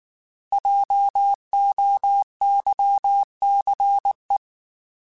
Morse code, heard as a keyed tone, about 19 wpm, 800 Hz. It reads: JOYCE
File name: Alphabetical / Alphabetical mp3s / Js @18